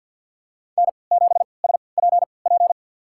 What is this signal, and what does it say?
Morse code, heard as a keyed tone, about 53 wpm, spaced as for 29 wpm, 700 Hz. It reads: N7SPP
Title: USA Call Signs - 5 Characters - 3x Character Spacing - 50wpm